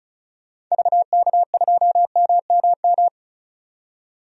Morse code, heard as a keyed tone, about 36 wpm, 700 Hz. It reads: VK2MMM
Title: DXCC-100 Call Signs - 6 Characters - 35wpm